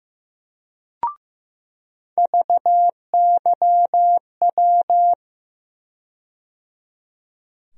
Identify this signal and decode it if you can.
Morse code, heard as a keyed tone, about 15 wpm, 700 Hz. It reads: VYW
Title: USA Call Sign Postfixes - 3 Letters 15wpm